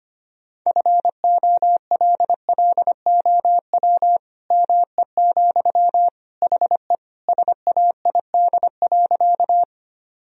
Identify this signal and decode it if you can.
Morse code, heard as a keyed tone, about 25 wpm, 700 Hz. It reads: FOLLOW ME, HE SAID.
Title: Sentences from Top 1000 Words - Review Part 1 - 25wpm